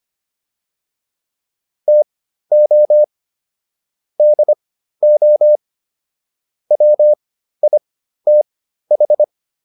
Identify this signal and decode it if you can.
Morse code, heard as a keyed tone, about 25 wpm, spaced as for 13 wpm, 600 Hz. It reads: TO DO WITH